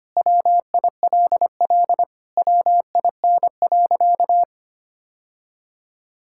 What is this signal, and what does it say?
Morse code, heard as a keyed tone, about 25 wpm, 700 Hz. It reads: WILL WIN.
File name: practice / 25wpm / Sentences with 3-Word Phrases - 25wpm